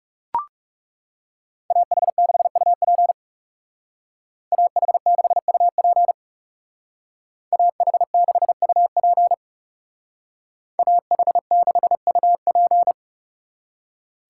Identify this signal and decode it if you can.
Morse code, heard as a keyed tone, about 36 wpm, 700 Hz. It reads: AH6UP AH6UP AH6UP AH6UP
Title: Speed Racing - USA 2x2 Call Signs 30wpm